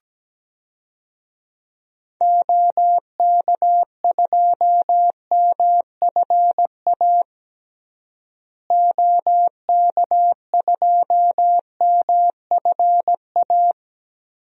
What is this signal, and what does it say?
Morse code, heard as a keyed tone, about 17 wpm, 700 Hz. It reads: OK2MFA OK2MFA
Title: DXCC-100 Call Signs - 6 Characters - Repeated 3x - 17wpm